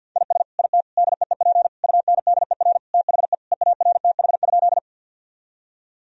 Morse code, heard as a keyed tone, about 47 wpm, 700 Hz. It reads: IS IT DEEP UNDER THE EARTH?
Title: Sentences from Top 1000 Words - Review Part 3 - 50wpm